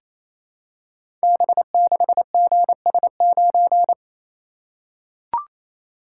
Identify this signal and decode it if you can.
Morse code, heard as a keyed tone, about 27 wpm, 700 Hz. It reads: B6GS9
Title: Code Groups - 5 Letter:Numbers - 28wpm